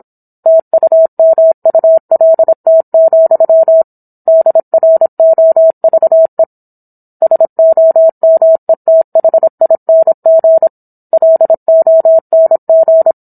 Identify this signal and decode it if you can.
Morse code, heard as a keyed tone, about 27 wpm, 650 Hz. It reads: TUMULT, DROVE SOMETHING LONG